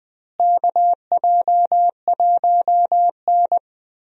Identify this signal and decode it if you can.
Morse code, heard as a keyed tone, about 20 wpm, 700 Hz. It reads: KJ1N
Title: USA Call Signs - 4 Characters - Repeated 3x - 20wpm